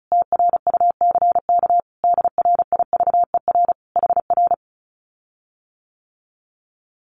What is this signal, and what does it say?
Morse code, heard as a keyed tone, about 34 wpm, 700 Hz. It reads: TRUCK DRIVER HR
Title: Common QSO Phrases Encore 35wpm